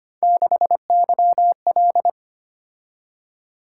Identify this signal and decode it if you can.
Morse code, heard as a keyed tone, about 25 wpm, 700 Hz. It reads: 6YL
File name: Three Letter-Number Combinations - 25wpm